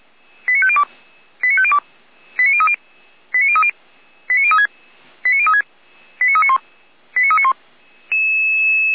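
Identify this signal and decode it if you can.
Five-tone selective call (ZVEI), heard as quick runs of five tones, tones of about 70 ms. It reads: 89502 89502 89030 89030 89036 89036 89391 89391
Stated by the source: ZVEI five-tone selective call received on SDR.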